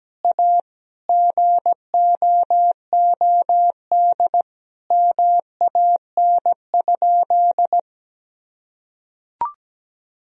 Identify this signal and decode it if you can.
Morse code, heard as a keyed tone, about 17 wpm, 700 Hz. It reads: A GOOD MAN?
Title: Sentences from Top 1000 Words - Review Part 1 - 17wpm